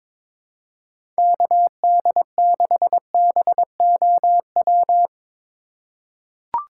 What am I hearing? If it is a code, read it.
Morse code, heard as a keyed tone, about 22 wpm, 700 Hz. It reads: KD6BOW
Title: USA Call Signs - 6 Characters - 4x Character Spacing - 22wpm